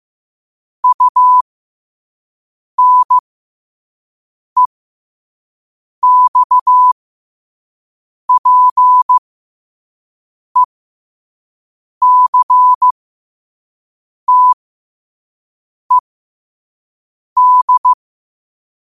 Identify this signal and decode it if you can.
Morse code, heard as a keyed tone, about 15 wpm, spaced as for 5 wpm, 1000 Hz. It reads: UNEXPECTED
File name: qotdfiles / 5wpm